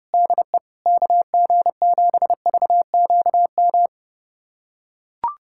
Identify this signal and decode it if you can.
Morse code, heard as a keyed tone, about 30 wpm, 700 Hz. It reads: DE KG7VQM
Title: USA Callsign DE Callsign 30wpm